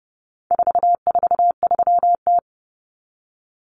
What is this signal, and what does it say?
Morse code, heard as a keyed tone, about 30 wpm, 700 Hz. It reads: 443T
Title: Common Cut Numbers - 4-Digit 30wpm